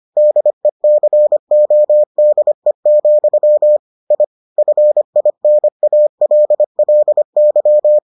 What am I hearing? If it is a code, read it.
Morse code, heard as a keyed tone, about 25 wpm, 600 Hz. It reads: DECODE, I FINALLY